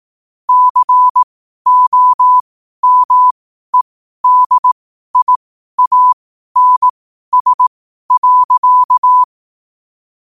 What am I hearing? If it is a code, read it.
Morse code, heard as a keyed tone, about 18 wpm, spaced as for 12 wpm, 1000 Hz. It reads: COMEDIANS.
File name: qotdfiles / 12wpm